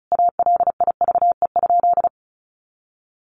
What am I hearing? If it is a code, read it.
Morse code, heard as a keyed tone, about 34 wpm, 700 Hz. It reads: ALIVE?
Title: A Sound Of Thunder 35wpm